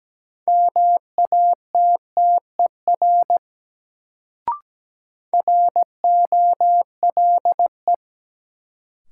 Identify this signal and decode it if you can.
Morse code, heard as a keyed tone, about 17 wpm, 700 Hz. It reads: MATTER ROLE
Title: Sets of 3 Words - New Word Drill - 17wpm